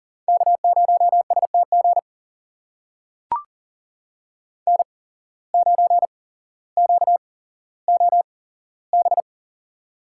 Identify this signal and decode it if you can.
Morse code, heard as a keyed tone, about 40 wpm, spaced as for 17 wpm, 700 Hz. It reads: K0STG N9QOB
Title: USA Call Signs - 5 Characters - 8x Character Spacing - 40wpm